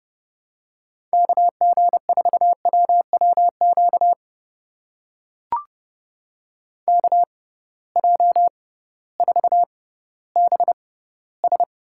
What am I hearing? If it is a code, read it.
Morse code, heard as a keyed tone, about 30 wpm, spaced as for 16 wpm, 700 Hz. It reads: KG4WWQ KJ4BS
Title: USA Call Signs - 6 Characters - 6x Character Spacing - 30wpm